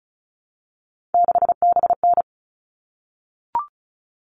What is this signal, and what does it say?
Morse code, heard as a keyed tone, about 34 wpm, 700 Hz. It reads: 6BN
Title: All Cut Numbers - 3-Digit 35wpm